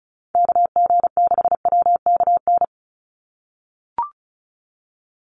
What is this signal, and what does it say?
Morse code, heard as a keyed tone, about 34 wpm, 700 Hz. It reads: KG6WKN
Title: USA Call Signs 35wpm